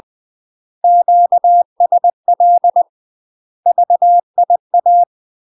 Morse code, heard as a keyed tone, about 20 wpm, 700 Hz. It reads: QSL VIA
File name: UK MASTER QSOs TRAINING / 20WPM / UK MASTER QSOs 0052